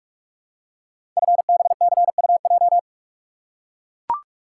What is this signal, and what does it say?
Morse code, heard as a keyed tone, about 44 wpm, 700 Hz. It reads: FBCUJ
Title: Code Groups - 5 Letters - 45wpm